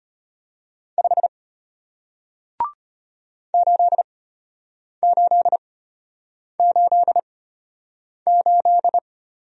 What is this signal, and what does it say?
Morse code, heard as a keyed tone, about 33 wpm, spaced as for 8 wpm, 700 Hz. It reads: 5 8888